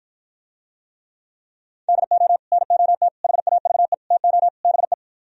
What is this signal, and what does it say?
Morse code, heard as a keyed tone, about 52 wpm, 700 Hz. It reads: DO NOT HAVE TO BE